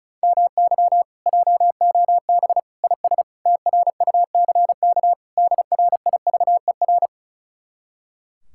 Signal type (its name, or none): Morse code